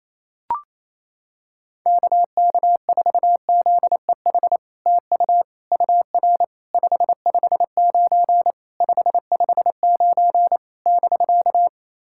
Morse code, heard as a keyed tone, about 28 wpm, 700 Hz. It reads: KK4ZEH TU UR 559 559 <BK>
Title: SOTA QSO Practice 28wpm